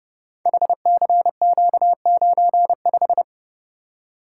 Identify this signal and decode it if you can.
Morse code, heard as a keyed tone, about 30 wpm, 700 Hz. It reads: HCQ95
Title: Code Groups - 5 Letter:Numbers - 30wpm